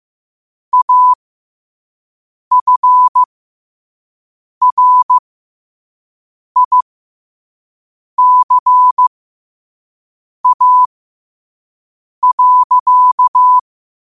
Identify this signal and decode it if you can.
Morse code, heard as a keyed tone, about 15 wpm, spaced as for 5 wpm, 1000 Hz. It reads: AFRICA.
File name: qotdfiles / 5wpm